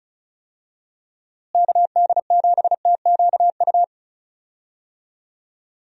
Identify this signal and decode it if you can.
Morse code, heard as a keyed tone, about 34 wpm, 700 Hz. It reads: KD7TQU
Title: USA Call Signs - 6 Characters - Repeated 3x - 35wpm